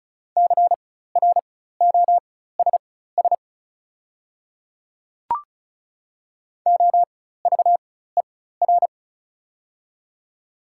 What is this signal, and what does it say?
Morse code, heard as a keyed tone, about 36 wpm, spaced as for 16 wpm, 700 Hz. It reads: CROSS OVER
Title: 4x Character Spacing - Top 500 Words - 35wpm